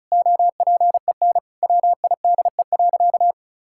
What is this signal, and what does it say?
Morse code, heard as a keyed tone, about 36 wpm, 700 Hz. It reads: OPEN WIDE.
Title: Sentences from Top 1000 Words - Review Part 6 - 35wpm